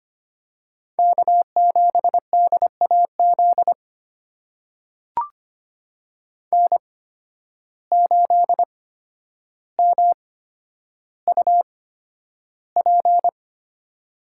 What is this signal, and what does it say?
Morse code, heard as a keyed tone, about 25 wpm, spaced as for 11 wpm, 700 Hz. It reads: K7DAZ N8MUP
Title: USA Call Signs - 5 Characters - 8x Character Spacing - 25wpm